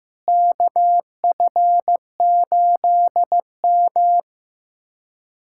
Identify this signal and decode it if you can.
Morse code, heard as a keyed tone, about 15 wpm, 700 Hz. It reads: KF8M
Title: USA Callsign DE Callsign - 4 Characters - 15wpm